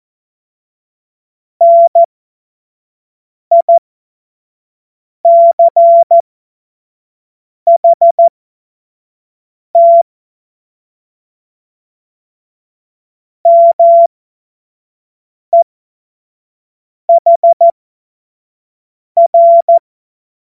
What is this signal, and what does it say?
Morse code, heard as a keyed tone, about 14 wpm, spaced as for 5 wpm, 680 Hz. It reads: NICHT MEHR
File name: Morsekurs / Witze / Witz 21. März 2023 25 F70